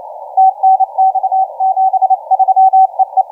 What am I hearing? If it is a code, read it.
Morse code, heard as a keyed tone, about 28 wpm, 750 Hz, with radio noise. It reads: TNX 73 EE
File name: 7S6W 0740UTC 17m